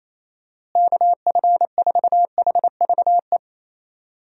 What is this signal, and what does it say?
Morse code, heard as a keyed tone, about 27 wpm, 700 Hz. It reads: KF4HVE
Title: USA Call Signs - 6 Characters - Repeated 2x - 28wpm